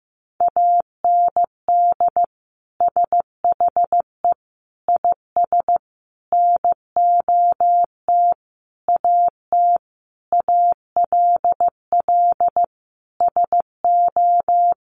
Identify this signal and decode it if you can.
Morse code, heard as a keyed tone, about 15 wpm, 700 Hz. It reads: AND SHE IS NOT AT ALL SO
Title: Sentences from Top 600 Words 15wpm